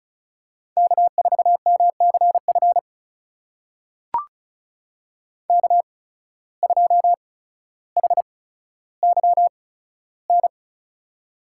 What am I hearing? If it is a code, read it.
Morse code, heard as a keyed tone, about 36 wpm, spaced as for 15 wpm, 700 Hz. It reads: K4MCF K2HYN
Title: USA Call Signs - 5 Characters - 8x Character Spacing - 35wpm